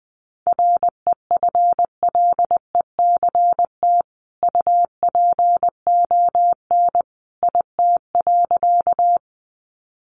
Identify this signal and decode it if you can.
Morse code, heard as a keyed tone, about 20 wpm, 700 Hz. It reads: REFLECT UPON IT.
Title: Sentences from Top 1500 Words 20wpm